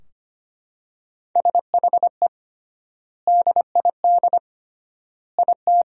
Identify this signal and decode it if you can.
Morse code, heard as a keyed tone, about 25 wpm, 700 Hz. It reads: SHE DID IT
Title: 3x Word-Spacing - Sentences from Top 400 Words - 25wpm